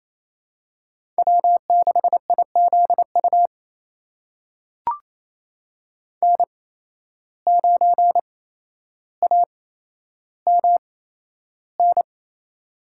Morse code, heard as a keyed tone, about 28 wpm, spaced as for 12 wpm, 700 Hz. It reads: W6IZU N9AMN
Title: USA Call Signs - 5 Characters - 8x Character Spacing - 28wpm